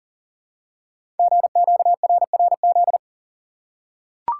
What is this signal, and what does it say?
Morse code, heard as a keyed tone, about 38 wpm, 700 Hz. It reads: GQRRZ
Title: Code Groups - 5 Letters - 40wpm